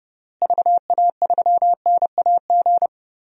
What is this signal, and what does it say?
Morse code, heard as a keyed tone, about 32 wpm, 700 Hz. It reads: VA3NAG
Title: DXCC-100 Call Signs - 6 Characters - Repeated 3x - 30wpm